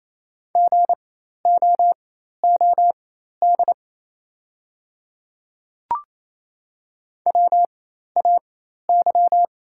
Morse code, heard as a keyed tone, about 28 wpm, spaced as for 13 wpm, 690 Hz. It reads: GOOD WAY
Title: 4x Character Spacing - Top 200 Words - 28wpm